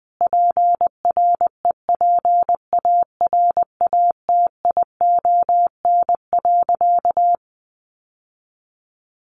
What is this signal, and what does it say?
Morse code, heard as a keyed tone, about 20 wpm, 700 Hz. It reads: PREPARATION.
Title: Sentences from Top 1400 Words 20wpm